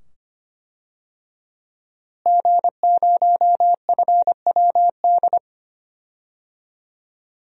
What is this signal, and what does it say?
Morse code, heard as a keyed tone, about 25 wpm, 700 Hz. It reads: G0FWD
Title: DXCC-100 Call Signs - 5 Characters - Repeated 2x - 25wpm